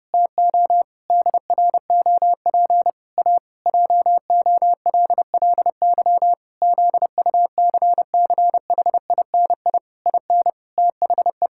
Morse code, heard as a keyed tone, about 30 wpm, 700 Hz. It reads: TO DROP A JOLLY ZUCCHINI IN THE
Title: Pangram Challenge v2 - 30wpm